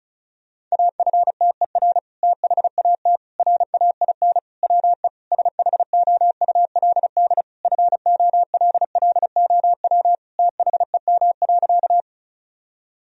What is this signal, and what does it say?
Morse code, heard as a keyed tone, about 34 wpm, 700 Hz. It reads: AFTER THAT RAIN WE SHOULD FOLLOW THEM.
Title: Sentences from Top 1000 Words - Review Part 1 - 35wpm